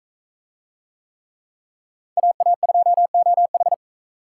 Morse code, heard as a keyed tone, about 40 wpm, 700 Hz. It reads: AA2OH